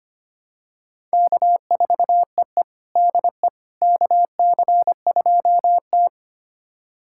Morse code, heard as a keyed tone, about 25 wpm, 700 Hz. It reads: K4EE DE KC2T
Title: USA Callsign DE Callsign - 4 Characters - 25wpm